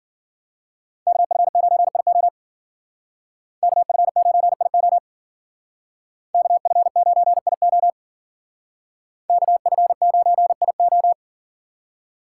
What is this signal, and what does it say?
Morse code, heard as a keyed tone, about 48 wpm, 700 Hz. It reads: KF9IO KF9IO KF9IO KF9IO
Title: Speed Racing - USA 2x2 Call Signs 40wpm